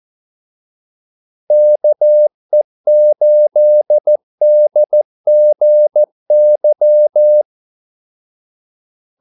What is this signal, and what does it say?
Morse code, heard as a keyed tone, about 14 wpm, 600 Hz. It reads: KE8DGY
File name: Callsigns-14wpm